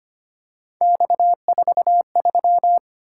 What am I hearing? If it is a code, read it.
Morse code, heard as a keyed tone, about 25 wpm, 700 Hz. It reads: X43
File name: Three Letter-Number Combinations - 25wpm